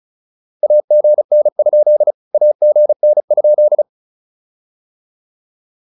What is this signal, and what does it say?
Morse code, heard as a keyed tone, about 34 wpm, 600 Hz. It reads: AGN? AGN?
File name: QSO106 35